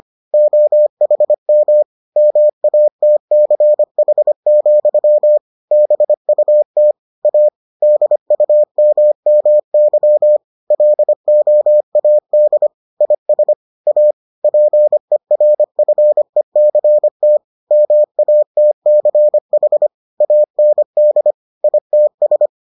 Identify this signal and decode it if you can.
Morse code, heard as a keyed tone, about 25 wpm, 600 Hz. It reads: OHM MATCH, BUT A DUMMY LOAD IS A PERFECT MATCH AND ITS